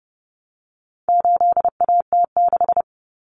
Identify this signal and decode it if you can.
Morse code, heard as a keyed tone, about 30 wpm, 700 Hz. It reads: 8AT6
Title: Common Cut Numbers - 4-Digit 30wpm